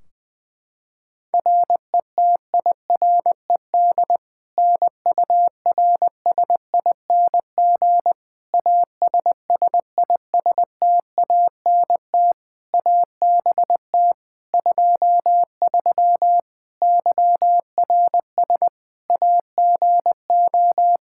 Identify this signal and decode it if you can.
Morse code, heard as a keyed tone, about 21 wpm, 700 Hz. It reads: RETIRED NURSING ASSISTANT ABT 23 YRS AGO
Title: Practice QSO - Exchange 3 - More Info - 20wpm